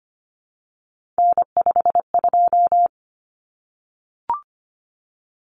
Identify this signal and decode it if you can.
Morse code, heard as a keyed tone, about 25 wpm, 700 Hz. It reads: N52
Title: Common Cut Numbers - 3-Digit 25wpm